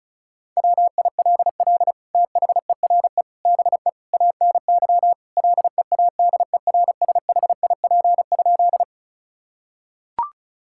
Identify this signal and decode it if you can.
Morse code, heard as a keyed tone, about 36 wpm, 700 Hz. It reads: WILL THERE BE ANY LEADERSHIP?
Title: Sentences from Top 1600 Words 35wpm